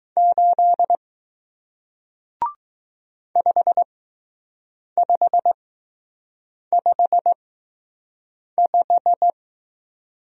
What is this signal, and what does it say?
Morse code, heard as a keyed tone, about 20 wpm, spaced as for 6 wpm, 700 Hz. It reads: 8 5555